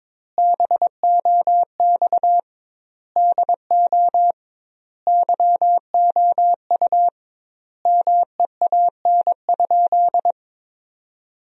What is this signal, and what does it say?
Morse code, heard as a keyed tone, about 22 wpm, 700 Hz. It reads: BOX DO YOU MEAN?
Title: 2x Word-Spacing - Sentences from Top 500 Words - 22wpm